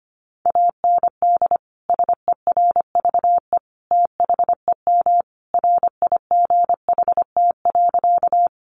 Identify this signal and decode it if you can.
Morse code, heard as a keyed tone, about 25 wpm, 700 Hz. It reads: AND SERVE THEM RIGHT.
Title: Sentences from Top 400 Words 25wpm